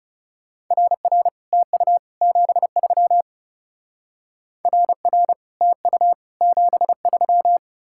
Morse code, heard as a keyed tone, about 34 wpm, 700 Hz. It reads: RR TU 73 RR TU 73
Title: Speed-Racing Common QSO Phrases - 30wpm